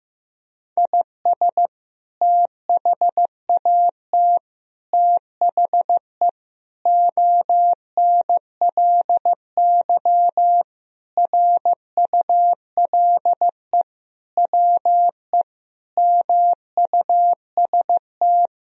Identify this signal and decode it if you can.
Morse code, heard as a keyed tone, about 15 wpm, 700 Hz. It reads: IS THAT THE ONLY RULE WE MUST
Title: Sentences from Top 1000 Words - Review Part 2 - 15wpm